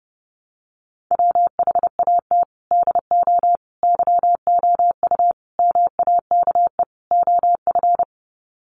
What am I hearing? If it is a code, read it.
Morse code, heard as a keyed tone, about 30 wpm, 700 Hz. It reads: WHAT DO YOU MAKE OF
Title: Sets of 5 Words 30wpm